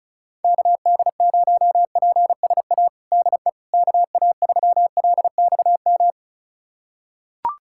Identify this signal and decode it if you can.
Morse code, heard as a keyed tone, about 34 wpm, 700 Hz. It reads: KD0PSA DE KA3LXM
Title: USA Callsign DE Callsign - 6 Characters - 35wpm